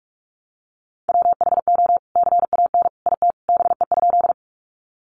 Morse code, heard as a keyed tone, about 44 wpm, 700 Hz. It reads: WHO CAN IT BE?
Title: Sentences from Top 100 Words 45wpm